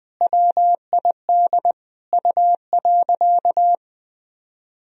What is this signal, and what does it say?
Morse code, heard as a keyed tone, about 21 wpm, 700 Hz. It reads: WID U.
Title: Sentences from Top 300 Words and Abbreviations 20wpm